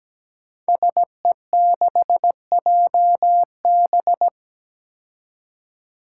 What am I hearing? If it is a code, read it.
Morse code, heard as a keyed tone, about 17 wpm, 700 Hz. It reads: SE6JB
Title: Five Challenging Letter-Number Combinations 17wpm